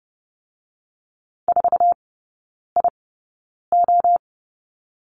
Morse code, heard as a keyed tone, about 30 wpm, 700 Hz. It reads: 4 I O
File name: 081 Lesson 14e - Number 2 - Three Characters 30wpm